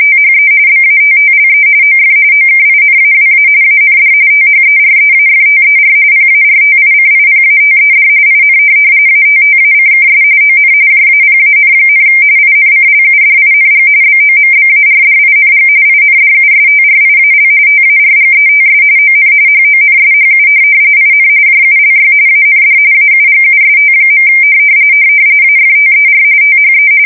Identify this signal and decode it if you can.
RTTY, about 45.45 baud. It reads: THE BATTLE OF OKINAWA AND THEREAFTER PARTICIPATED IN ATTACKS ON JAPAN, INCLUDING BOMBARDING INDUSTRIAL TARGETS ON HONSHU IN JULY AND AUGUST. AFTER THE
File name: fsk170 / USS-Massachusetts